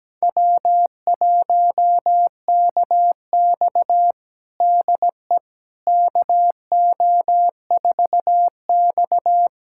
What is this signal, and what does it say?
Morse code, heard as a keyed tone, about 17 wpm, 700 Hz. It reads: W1KX DE KO4X
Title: USA Callsign DE Callsign - 4 Characters - 17wpm